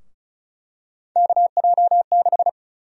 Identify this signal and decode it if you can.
Morse code, heard as a keyed tone, about 34 wpm, 700 Hz. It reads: KJ6
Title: USA Call Sign Prefixes - Repeated 2x - 35wpm